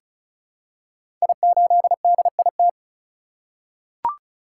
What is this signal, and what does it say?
Morse code, heard as a keyed tone, about 36 wpm, 700 Hz. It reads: I8DIT
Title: Code Groups - 5 Letter:Numbers - 35wpm